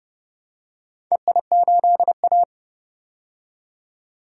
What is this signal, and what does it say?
Morse code, heard as a keyed tone, about 31 wpm, 700 Hz. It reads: EI8A
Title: DXCC-100 Call Signs - 4 Characters - 30wpm